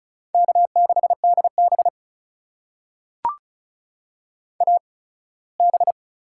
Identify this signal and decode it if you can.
Morse code, heard as a keyed tone, about 36 wpm, spaced as for 21 wpm, 700 Hz. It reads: K6DB AB
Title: USA Call Signs - 4 Characters - 8x Character Spacing - 35wpm